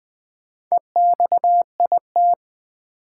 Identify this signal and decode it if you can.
Morse code, heard as a keyed tone, about 20 wpm, 700 Hz. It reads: EXIT
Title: Sentences from Top 1100 Words - New Word Drill Alt - 20wpm